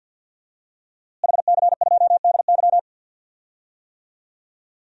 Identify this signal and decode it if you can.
Morse code, heard as a keyed tone, about 52 wpm, 700 Hz. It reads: HC2DY
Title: DXCC-100 Call Signs - 5 Characters - Repeated 3x - 50wpm